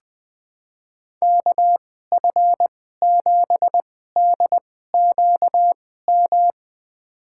Morse code, heard as a keyed tone, about 20 wpm, spaced as for 14 wpm, 700 Hz. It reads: KF7DQM
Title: USA Call Signs - 6 Characters - 2x Character Spacing - 20wpm